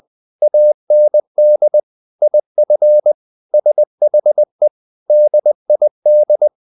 Morse code, heard as a keyed tone, about 20 wpm, 600 Hz. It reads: AND IF SHE DID
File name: the old man and the sea / the old man and the sea part 1 20wpm